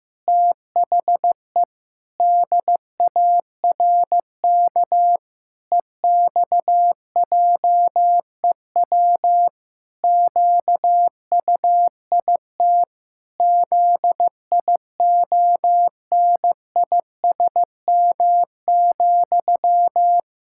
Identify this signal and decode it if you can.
Morse code, heard as a keyed tone, about 15 wpm, 700 Hz. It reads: THE DARK EXJEW QUIT ZIONISM,
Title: Pangram Challenge v2 - 15wpm